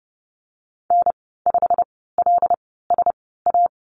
Morse code, heard as a keyed tone, about 30 wpm, 700 Hz. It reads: N 5 L S A
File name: Lesson 19g - Letter w - Call Signs 30wpm